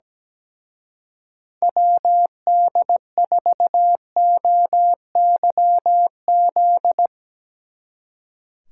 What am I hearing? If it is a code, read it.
Morse code, heard as a keyed tone, about 19 wpm, 700 Hz. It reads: WD4OYZ
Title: Speed Racing - USA 2x3 Call Signs 17wpm